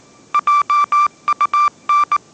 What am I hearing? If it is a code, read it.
Morse code, heard as a keyed tone, about 19 wpm, 1180 Hz. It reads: JUN
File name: mrx083 20wpm